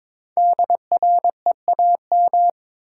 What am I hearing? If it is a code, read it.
Morse code, heard as a keyed tone, about 22 wpm, 700 Hz. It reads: DREAM
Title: Sentences from Top 1500 Words - New Word Drill - 22wpm